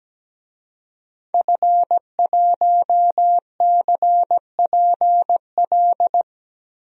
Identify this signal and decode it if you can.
Morse code, heard as a keyed tone, about 17 wpm, 700 Hz. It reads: F1CPL
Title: DXCC-100 Call Signs - 5 Characters - Repeated 2x - 17wpm